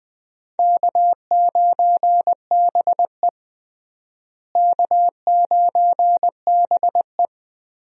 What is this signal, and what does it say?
Morse code, heard as a keyed tone, about 20 wpm, 700 Hz. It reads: K9BE K9BE
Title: USA Call Signs - 4 Characters - Repeated 3x - 20wpm